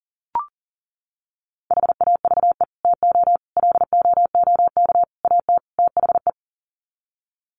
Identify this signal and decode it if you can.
Morse code, heard as a keyed tone, about 37 wpm, 700 Hz. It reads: HAVE TO LOOK AT THE
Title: Sets of 5 Words Encore 40wpm